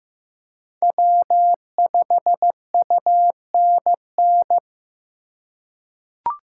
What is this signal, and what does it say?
Morse code, heard as a keyed tone, about 15 wpm, 700 Hz. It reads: W5UNN
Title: USA Call Signs - 5 Characters - 2x Character Spacing - 15wpm